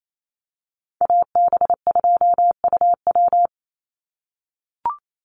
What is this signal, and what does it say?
Morse code, heard as a keyed tone, about 28 wpm, 700 Hz. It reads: AB2UW
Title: USA Call Signs 28wpm